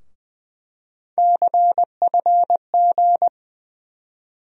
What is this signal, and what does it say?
Morse code, heard as a keyed tone, about 20 wpm, 700 Hz. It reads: CFG
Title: USA Call Sign Postfixes - 3 Letters 20wpm